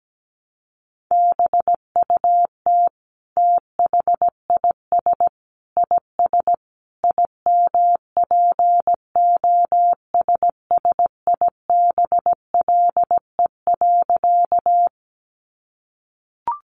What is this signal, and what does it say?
Morse code, heard as a keyed tone, about 17 wpm, 700 Hz. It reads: BUT THIS IS IMPOSSIBLE.
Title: Sentences from Top 1100 Words 17wpm